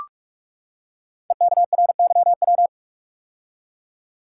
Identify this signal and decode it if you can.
Morse code, heard as a keyed tone, about 44 wpm, 700 Hz. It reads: EKRYW
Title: Code Groups - 5 Letters - 45wpm